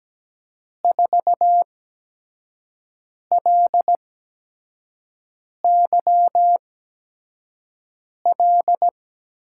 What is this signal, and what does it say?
Morse code, heard as a keyed tone, about 17 wpm, spaced as for 5 wpm, 700 Hz. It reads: 4LYL
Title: USA Call Signs - 8x Character Spacing - 17wpm